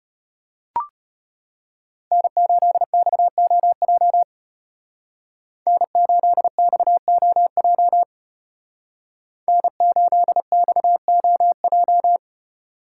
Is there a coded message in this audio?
Yes: Morse code from 2.1 to 12.2 s, heard as a keyed tone, about 34 wpm, 700 Hz. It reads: N8XOJ N8XOJ N8XOJ